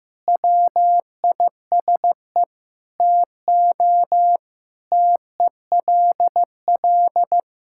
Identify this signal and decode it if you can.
Morse code, heard as a keyed tone, about 15 wpm, 700 Hz. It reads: WISE TO TELL
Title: Sentences from Top 1600 Words 15wpm